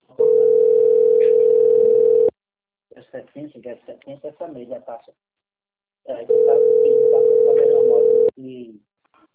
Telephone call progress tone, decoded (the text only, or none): ringback tone